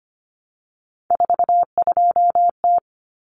Morse code, heard as a keyed tone, about 25 wpm, 700 Hz. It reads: 42T
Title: Common Cut Numbers - 3-Digit 25wpm